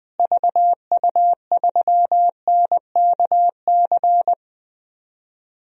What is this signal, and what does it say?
Morse code, heard as a keyed tone, about 20 wpm, 700 Hz. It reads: VU3NKC
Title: DXCC-100 Call Signs - 6 Characters - Repeated 3x - 20wpm